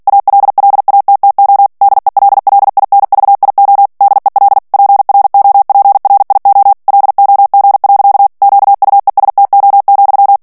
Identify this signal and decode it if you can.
Morse code, heard as a keyed tone, about 48 wpm, 800 Hz. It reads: ALLATTO DELLINVIO DEL PROPRIO LOG. QUESTO,
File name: ITALIEN texte clair 2 vitesse 45 mots minute